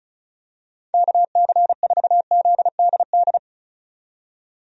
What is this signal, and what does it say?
Morse code, heard as a keyed tone, about 36 wpm, 700 Hz. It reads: KC4ZDD
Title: USA Call Signs - 6 Characters - 35wpm